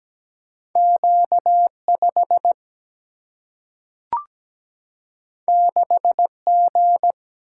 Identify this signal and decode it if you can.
Morse code, heard as a keyed tone, about 17 wpm, 700 Hz. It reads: Q5 6G
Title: Single-Letter Single-Number 17wpm